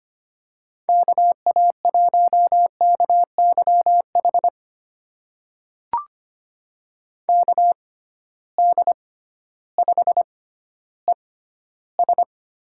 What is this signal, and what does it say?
Morse code, heard as a keyed tone, about 25 wpm, spaced as for 14 wpm, 700 Hz. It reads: KA1KYH KD5ES
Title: USA Call Signs - 6 Characters - 6x Character Spacing - 25wpm